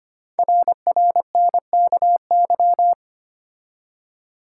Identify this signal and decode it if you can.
Morse code, heard as a keyed tone, about 25 wpm, 700 Hz. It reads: RRNKY
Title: Code Groups - 5 Letters - 25wpm